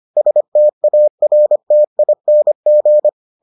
Morse code, heard as a keyed tone, about 25 wpm, 600 Hz. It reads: STARTING